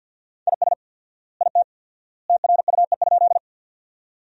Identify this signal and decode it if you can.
Morse code, heard as a keyed tone, about 50 wpm, 700 Hz. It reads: IS IT TRUE?
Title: 4x Word-Spacing - Sentences from Top 400 Words - 50wpm